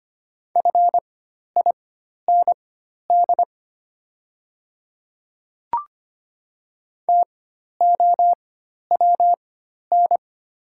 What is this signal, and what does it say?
Morse code, heard as a keyed tone, about 25 wpm, spaced as for 12 wpm, 690 Hz. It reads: FIND TOWN
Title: 4x Character Spacing - Top 500 Words - 25wpm